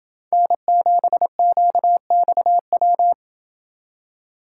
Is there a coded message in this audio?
Yes: Morse code from 0.3 to 3.1 s, heard as a keyed tone, about 27 wpm, 700 Hz. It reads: N7QXW